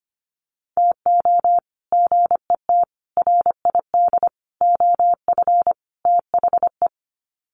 Morse code, heard as a keyed tone, about 25 wpm, 700 Hz. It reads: TO GET RID OF THE
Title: Sets of 5 Words 25wpm